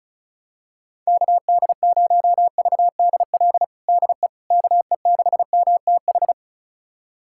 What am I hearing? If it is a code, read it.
Morse code, heard as a keyed tone, about 36 wpm, 700 Hz. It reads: KD0VDL DE KE6MTH
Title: USA Callsign DE Callsign - 6 Characters - 35wpm